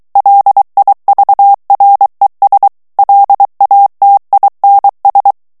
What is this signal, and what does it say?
Morse code, heard as a keyed tone, about 23 wpm, 800 Hz. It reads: LIVRES LATINS
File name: morse mp3 / FRANCAIS TEXTE CLAIR 2 VITESSE 22 MOTS MINUTES